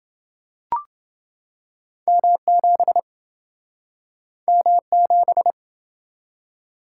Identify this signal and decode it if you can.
Morse code, heard as a keyed tone, about 28 wpm, 700 Hz. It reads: M7 M7